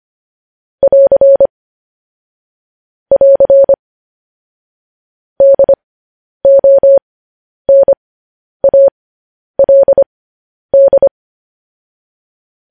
Morse code, heard as a keyed tone, about 25 wpm, spaced as for 10 wpm, 550 Hz. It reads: <AR> <AR> DONALD